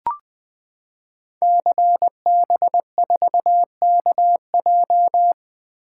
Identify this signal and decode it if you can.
Morse code, heard as a keyed tone, about 20 wpm, 700 Hz. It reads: CB4KJ
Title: Code Groups - 5 Letter:Numbers - 20wpm